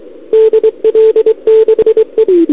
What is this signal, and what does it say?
Morse code, heard as a keyed tone, about 27 wpm, 440 Hz, with radio noise. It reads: DL6E
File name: CQWWCW2021 / Doderna OL3Z